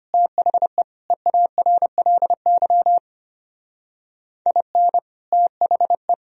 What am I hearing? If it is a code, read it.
Morse code, heard as a keyed tone, about 31 wpm, 700 Hz. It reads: THE EARLY IN THE
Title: Speed-Racing - Sets of 3 Words 25wpm